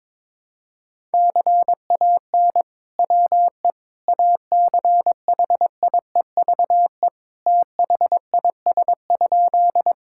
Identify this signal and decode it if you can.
Morse code, heard as a keyed tone, about 22 wpm, 700 Hz. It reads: CAN WE ACHIEVE THIS?
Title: Sentences from Top 1000 Words - Review Part 6 - 22wpm